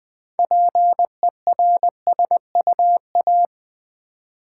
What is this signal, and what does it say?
Morse code, heard as a keyed tone, about 20 wpm, 700 Hz. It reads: PERSUA
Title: Build A Word - 3 Letters At A Time - 8-Letter Words - 20wpm